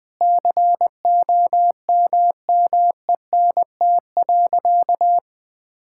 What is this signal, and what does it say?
Morse code, heard as a keyed tone, about 20 wpm, 700 Hz. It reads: COMMENT.
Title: Sentences with 8-Word Phrases - 20wpm